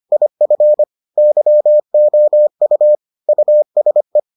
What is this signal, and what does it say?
Morse code, heard as a keyed tone, about 25 wpm, 600 Hz. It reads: IF YOU USE